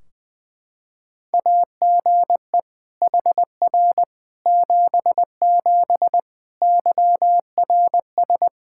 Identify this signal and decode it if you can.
Morse code, heard as a keyed tone, about 21 wpm, 700 Hz. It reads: AGE HR 77 YRS
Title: Practice QSO - Exchange 3 - More Info - 20wpm